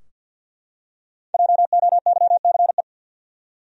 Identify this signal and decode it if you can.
Morse code, heard as a keyed tone, about 53 wpm, 700 Hz. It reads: JOYCE
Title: USA Common Names - 50wpm